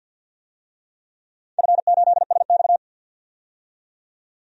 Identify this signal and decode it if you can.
Morse code, heard as a keyed tone, about 52 wpm, 700 Hz. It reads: F8SX
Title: DXCC-100 Call Signs - 4 Characters - 50wpm